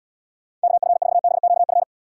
Morse code, heard as a keyed tone, about 78 wpm, 700 Hz. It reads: KF4BQF